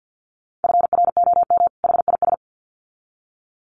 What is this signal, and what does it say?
Morse code, heard as a keyed tone, about 50 wpm, 700 Hz. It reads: FROM HIS
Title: Sets of 2 Words 50wpm